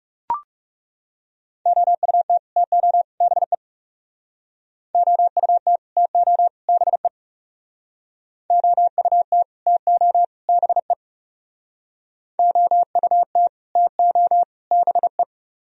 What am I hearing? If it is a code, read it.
Morse code, heard as a keyed tone, about 37 wpm, 700 Hz. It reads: OUT TO BE OUT TO BE OUT TO BE OUT TO BE
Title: Speed-Racing - Sets of 3 Words 30wpm